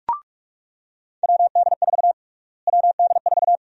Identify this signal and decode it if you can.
Morse code, heard as a keyed tone, about 45 wpm, 700 Hz. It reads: WD4 WD4
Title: USA Call Sign Prefixes - Repeated 2x - 45wpm